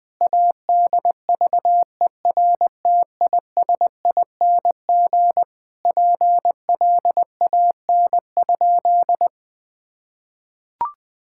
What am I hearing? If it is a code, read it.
Morse code, heard as a keyed tone, about 20 wpm, 700 Hz. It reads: ADVERTISING PLAN?
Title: Sentences from Top 1000 Words - Review Part 6 - 20wpm